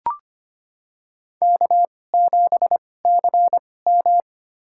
Morse code, heard as a keyed tone, about 25 wpm, spaced as for 18 wpm, 700 Hz. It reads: K7CM
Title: USA Call Signs - 4 Characters - 2x Character Spacing - 25wpm